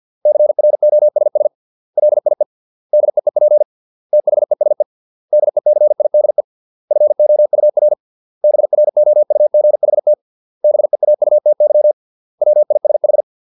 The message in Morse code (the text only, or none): CROSS LIE DEEP THESE DECIDE FOUR BROUGHT BEAUTY WISH